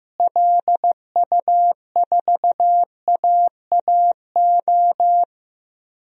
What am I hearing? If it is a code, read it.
Morse code, heard as a keyed tone, about 15 wpm, 700 Hz. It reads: LU4AAO
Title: DXCC-100 Call Signs - 6 Characters - Repeated 3x - 15wpm